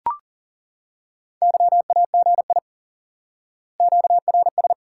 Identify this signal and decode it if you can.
Morse code, heard as a keyed tone, about 37 wpm, 700 Hz. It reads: YAGI QRS
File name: IQE Course 3 - Lesson 89 - YAGI - Two Elements - 40wpm